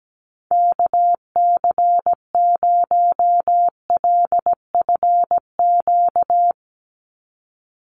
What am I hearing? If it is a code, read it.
Morse code, heard as a keyed tone, about 17 wpm, 700 Hz. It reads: KC0LFQ
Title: USA Call Signs 17wpm